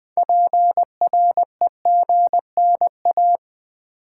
Morse code, heard as a keyed tone, about 20 wpm, 700 Hz. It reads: PREGNA
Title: Build A Word - 3 Letters At A Time - 9-Letter Words - 20wpm